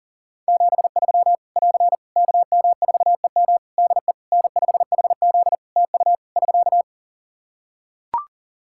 Morse code, heard as a keyed tone, about 37 wpm, 700 Hz. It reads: 73 <AR> KM4EM DE N5HZ TU <SK>
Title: Practice QSO - Exchange 4 - Ending - 40wpm